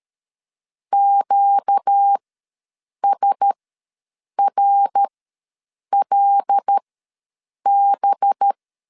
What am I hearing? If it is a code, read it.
Morse code, heard as a keyed tone, about 13 wpm, spaced as for 7 wpm, 800 Hz. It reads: QSRLB
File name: LEZIONI / 5 LBVC / 5 LB ws 12wpm 3034